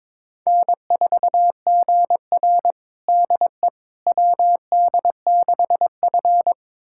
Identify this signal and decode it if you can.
Morse code, heard as a keyed tone, about 22 wpm, 700 Hz. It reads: N4GR DE WD6F
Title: USA Callsign DE Callsign - 4 Characters - 22wpm